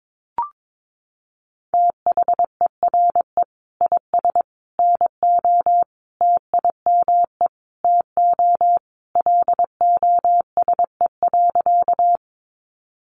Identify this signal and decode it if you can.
Morse code, heard as a keyed tone, about 22 wpm, 700 Hz. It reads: THERE IS NO TIME TO LOSE.
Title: Sentences from Top 1100 Words 22wpm